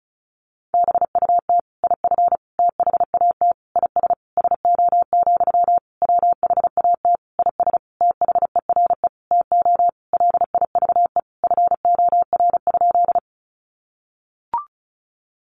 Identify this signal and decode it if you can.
Morse code, heard as a keyed tone, about 34 wpm, 700 Hz. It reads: BUT IF THAT IS SO, WHAT IS THERE TO LIVE FOR?
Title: Sentences from Top 200 Words 35wpm